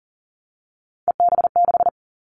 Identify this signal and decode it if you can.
Morse code, heard as a keyed tone, about 42 wpm, 700 Hz. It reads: EB6
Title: Callsign Prefixes for Top 100 DXCC - 40wpm